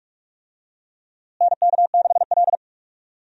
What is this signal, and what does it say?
Morse code, heard as a keyed tone, about 42 wpm, 700 Hz. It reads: NK6L
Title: USA Call Signs - 4 Characters - Repeated 2x - 45wpm